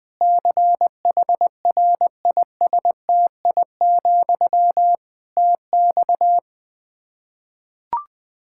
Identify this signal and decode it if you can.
Morse code, heard as a keyed tone, about 21 wpm, 700 Hz. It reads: CHRISTI, TX
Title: K1USN SST QSO Practice 20wpm